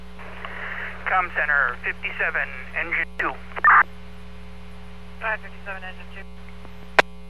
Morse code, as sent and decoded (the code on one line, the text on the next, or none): none